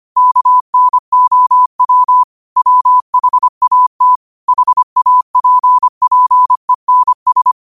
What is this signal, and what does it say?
Morse code, heard as a keyed tone, about 25 wpm, 1000 Hz. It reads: KNOW WHAT HAPPENS